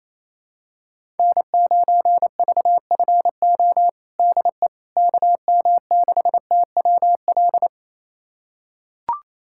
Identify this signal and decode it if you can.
Morse code, heard as a keyed tone, about 28 wpm, 700 Hz. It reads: N9VFO DE KM6TWL
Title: USA Callsign DE Callsign 28wpm